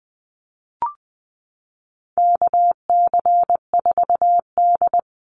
Morse code, heard as a keyed tone, about 20 wpm, 700 Hz. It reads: KC4D
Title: Call Signs Repeated 3x - 20wpm